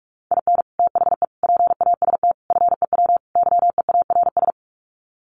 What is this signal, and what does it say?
Morse code, heard as a keyed tone, about 44 wpm, 700 Hz. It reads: IN THE PAST FEW YEARS
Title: Sets of 5 Words 45wpm